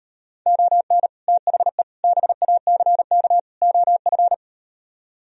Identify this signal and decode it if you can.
Morse code, heard as a keyed tone, about 38 wpm, 700 Hz. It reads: ON THE BACK OF